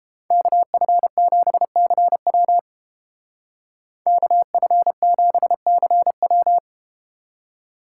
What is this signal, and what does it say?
Morse code, heard as a keyed tone, about 32 wpm, 700 Hz. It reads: KF7CW KF7CW